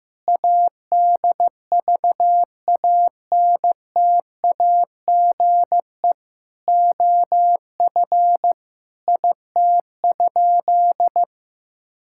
Morse code, heard as a keyed tone, about 15 wpm, 700 Hz. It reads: ADVANTAGE OF IT?
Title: Sentences from Top 1000 Words - Review Part 6 - 15wpm